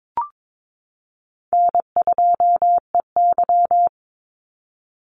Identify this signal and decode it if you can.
Morse code, heard as a keyed tone, about 22 wpm, 700 Hz. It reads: N2EY
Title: Easy Call Signs Repeated 3x - 22wpm